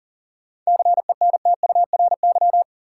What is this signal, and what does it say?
Morse code, heard as a keyed tone, about 40 wpm, 700 Hz. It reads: CENTURY
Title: Sets of 5 Words - New Word Drill - 40wpm